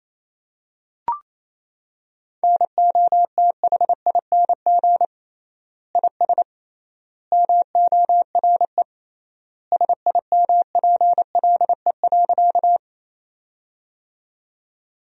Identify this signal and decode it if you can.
Morse code, heard as a keyed tone, about 28 wpm, 700 Hz. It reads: NOTHING IS MORE SIMPLE.
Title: 3x Word-Spacing - Sentences from Top 500 Words - 28wpm